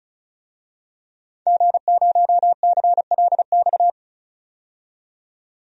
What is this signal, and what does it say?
Morse code, heard as a keyed tone, about 36 wpm, 700 Hz. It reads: G0CLX
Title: DXCC-100 Call Signs - 5 Characters - 35wpm